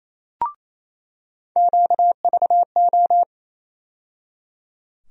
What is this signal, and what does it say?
Morse code, heard as a keyed tone, about 28 wpm, 700 Hz. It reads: QVO
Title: USA Call Sign Postfixes - 3 Letters 28wpm